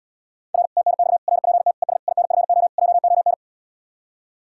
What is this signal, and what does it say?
Morse code, heard as a keyed tone, about 77 wpm, 700 Hz. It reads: R TNX RPT ES INFO JON